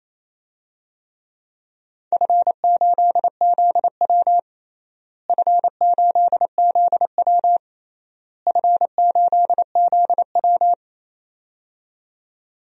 Morse code, heard as a keyed tone, about 27 wpm, 700 Hz. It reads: F8ZW F8ZW F8ZW
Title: DXCC-100 Call Signs - 4 Characters - Repeated 3x - 28wpm